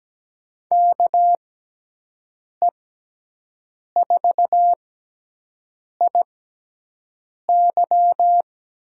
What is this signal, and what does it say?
Morse code, heard as a keyed tone, about 17 wpm, spaced as for 6 wpm, 700 Hz. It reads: KE4IY
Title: USA Call Signs - 6 Characters - 6x Character Spacing - 17wpm